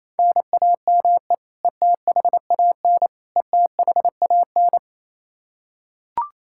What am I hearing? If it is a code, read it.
Morse code, heard as a keyed tone, about 27 wpm, 700 Hz. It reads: NAME ETHAN ETHAN
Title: Practice QSO - Exchange 1 - Beginning - 28wpm